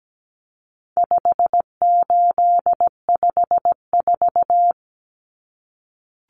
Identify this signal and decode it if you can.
Morse code, heard as a keyed tone, about 17 wpm, 700 Hz. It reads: 5854
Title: Common Cut Numbers - 4-Digit 17wpm